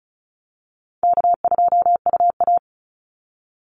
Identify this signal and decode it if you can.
Morse code, heard as a keyed tone, about 34 wpm, 700 Hz. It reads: K2UA
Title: Call Signs Repeated 2x - 35wpm